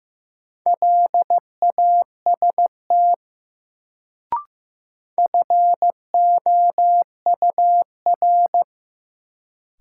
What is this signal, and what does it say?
Morse code, heard as a keyed tone, about 15 wpm, 700 Hz. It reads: LAST FOUR
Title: Sentences from Top 300 Words - New Word Drill - 15wpm